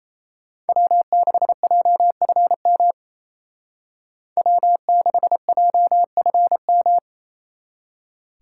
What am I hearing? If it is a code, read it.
Morse code, heard as a keyed tone, about 32 wpm, 700 Hz. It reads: W6JFM W6JFM
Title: Speed Racing - USA 1x3 Call Signs 28wpm